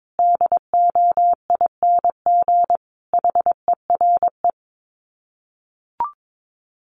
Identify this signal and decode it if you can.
Morse code, heard as a keyed tone, about 22 wpm, 700 Hz. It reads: DOING HERE
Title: Sets of 5 Words 22wpm